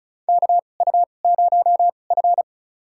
Morse code, heard as a keyed tone, about 36 wpm, spaced as for 25 wpm, 700 Hz. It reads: KU0F
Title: USA Call Signs - 2x Character Spacing - 35wpm